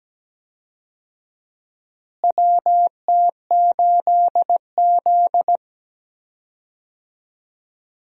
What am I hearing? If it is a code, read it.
Morse code, heard as a keyed tone, about 17 wpm, 700 Hz. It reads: WT8Z
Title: USA Call Signs - 4 Characters - Repeated 3x - 17wpm